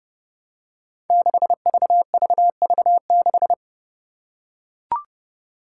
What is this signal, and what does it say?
Morse code, heard as a keyed tone, about 30 wpm, 700 Hz. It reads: 6VVV6
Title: Five Challenging Letter-Number Combinations 30wpm